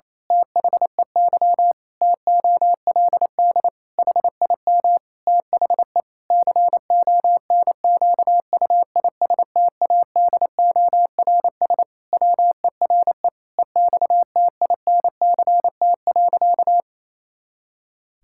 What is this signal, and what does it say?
Morse code, heard as a keyed tone, about 27 wpm, 700 Hz. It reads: THEY TOLD HIM THE CONQUISTADORS WERE EXTINCT.
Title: Pangram Challenge v2 - 28wpm